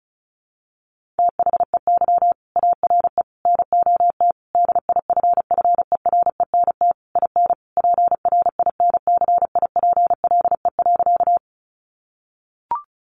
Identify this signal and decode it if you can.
Morse code, heard as a keyed tone, about 34 wpm, 700 Hz. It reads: THEY ARE NOT DIFFERENT IN PRINCIPLE.
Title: Sentences from Top 1400 Words 35wpm